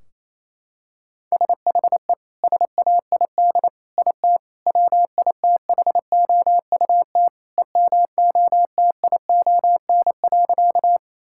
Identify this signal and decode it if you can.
Morse code, heard as a keyed tone, about 28 wpm, 700 Hz. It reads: SHE SAID IT WITHOUT EMOTION.
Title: Sentences from Top 1600 Words 28wpm